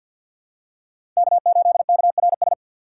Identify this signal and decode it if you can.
Morse code, heard as a keyed tone, about 52 wpm, 700 Hz. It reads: K8KRS
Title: USA Call Signs - 5 Characters - 3x Character Spacing - 50wpm